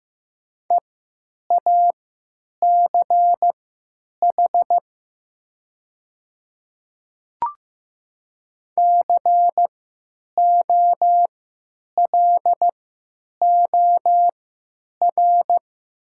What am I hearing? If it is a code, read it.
Morse code, heard as a keyed tone, about 15 wpm, spaced as for 9 wpm, 700 Hz. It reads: EACH COLOR
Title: 3x Character Spacing - Top 300 Words - 15wpm